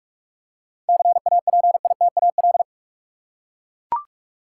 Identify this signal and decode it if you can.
Morse code, heard as a keyed tone, about 42 wpm, 700 Hz. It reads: CAPITAL
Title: Seven Letter Words - 45wpm